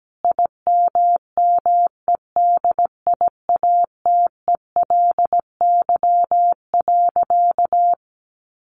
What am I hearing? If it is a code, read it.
Morse code, heard as a keyed tone, about 17 wpm, 700 Hz. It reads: IMMEDIATELY.
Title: Sentences from Top 1100 Words 17wpm